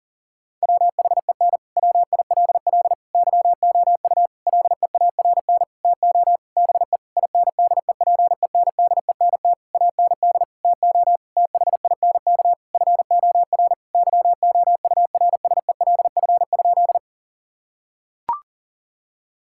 Morse code, heard as a keyed tone, about 40 wpm, 700 Hz. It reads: WHEN WILL YOU LEARN TO BE INDEPENDENT AND TO THINK FOR YOURSELF?
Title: Sentences from Top 1000 Words - Review Part 6 - 40wpm